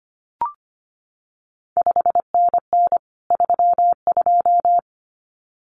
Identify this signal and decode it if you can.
Morse code, heard as a keyed tone, about 25 wpm, 700 Hz. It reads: 5NN 32
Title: CQ WW DX Contest Exchange Encore - 25wpm